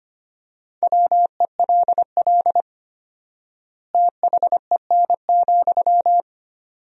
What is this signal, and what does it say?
Morse code, heard as a keyed tone, about 25 wpm, 700 Hz. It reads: WELL THEN,
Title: 4x Word-Spacing - Sentences from Top 200 Words - 25wpm